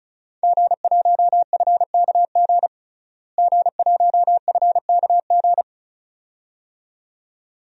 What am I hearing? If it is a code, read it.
Morse code, heard as a keyed tone, about 36 wpm, 700 Hz. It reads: G1FKG G1FKG
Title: DXCC-100 Call Signs - 5 Characters - Repeated 2x - 35wpm